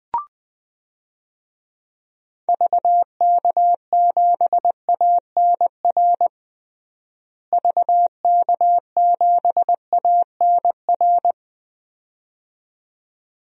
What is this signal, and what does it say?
Morse code, heard as a keyed tone, about 20 wpm, 700 Hz. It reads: VK7ANR VK7ANR
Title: DXCC-100 Call Signs - 6 Characters - Repeated 2x - 20wpm